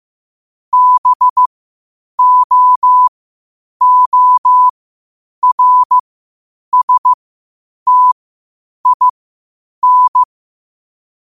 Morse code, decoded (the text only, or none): BOORSTIN